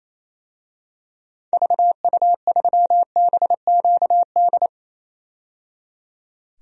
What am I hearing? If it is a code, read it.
Morse code, heard as a keyed tone, about 28 wpm, 700 Hz. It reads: VU3BQD
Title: DXCC-100 Call Signs - 6 Characters - 28wpm